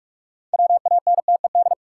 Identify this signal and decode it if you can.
Morse code, heard as a keyed tone, about 46 wpm, 700 Hz. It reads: WANTED
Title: Sets of 2 Words - New Word Drill - 45wpm